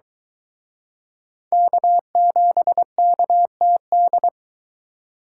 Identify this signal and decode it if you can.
Morse code, heard as a keyed tone, about 24 wpm, 700 Hz. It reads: K7KTD